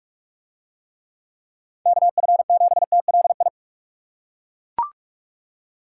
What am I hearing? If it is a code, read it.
Morse code, heard as a keyed tone, about 41 wpm, 700 Hz. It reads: KF7TLI